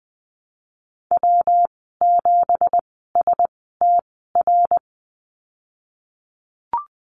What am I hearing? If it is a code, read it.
Morse code, heard as a keyed tone, about 20 wpm, spaced as for 15 wpm, 700 Hz. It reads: W7STR
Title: 173 Lesson 28g - Number 9 - Call Signs 20wpm